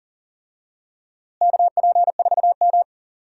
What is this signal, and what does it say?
Morse code, heard as a keyed tone, about 40 wpm, 700 Hz. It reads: KP4M
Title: DXCC-100 Call Signs - 4 Characters - Repeated 2x - 40wpm